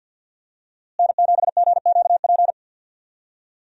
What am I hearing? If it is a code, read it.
Morse code, heard as a keyed tone, about 51 wpm, 700 Hz. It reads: N7GQP
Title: USA Call Signs - 5 Characters - Repeated 3x - 50wpm